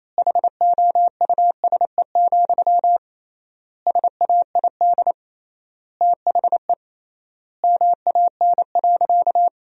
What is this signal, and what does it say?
Morse code, heard as a keyed tone, about 28 wpm, 700 Hz. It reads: HOUSE, SAID THE MAN.
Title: 3x Word-Spacing - Sentences from Top 400 Words - 28wpm